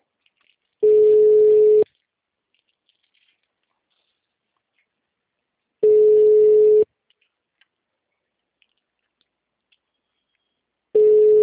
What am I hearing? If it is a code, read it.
Ringback tone (Europe, 425 Hz).
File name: out- -2051- - - .77746